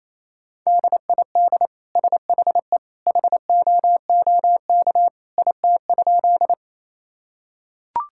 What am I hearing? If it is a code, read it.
Morse code, heard as a keyed tone, about 28 wpm, 700 Hz. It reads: DID SHE HOOK IT?
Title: Sentences from Top 1600 Words 28wpm